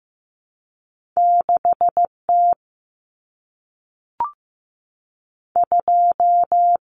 Morse code, heard as a keyed tone, about 15 wpm, 700 Hz. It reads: 6 T 2
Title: Common Cut Numbers - 2-Digit 15wpm